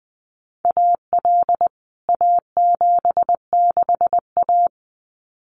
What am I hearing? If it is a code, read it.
Morse code, heard as a keyed tone, about 20 wpm, 700 Hz. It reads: AL A76A
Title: CWT Exchange 20wpm